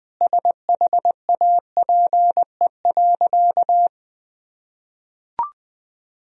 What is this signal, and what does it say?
Morse code, heard as a keyed tone, about 20 wpm, 700 Hz. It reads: SHAPE.
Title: Sentences from Top 1000 Words - Review Part 3 - 20wpm